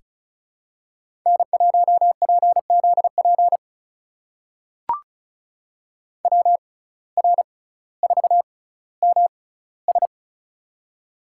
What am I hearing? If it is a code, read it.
Morse code, heard as a keyed tone, about 36 wpm, spaced as for 18 wpm, 700 Hz. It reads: N1PZP WR4MS
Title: USA Call Signs - 5 Characters - 6x Character Spacing - 35wpm